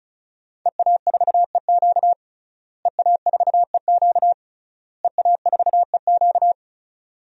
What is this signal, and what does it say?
Morse code, heard as a keyed tone, about 34 wpm, 700 Hz. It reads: EA4EQ EA4EQ EA4EQ
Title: DXCC-100 Call Signs - 5 Characters - Repeated 3x - 35wpm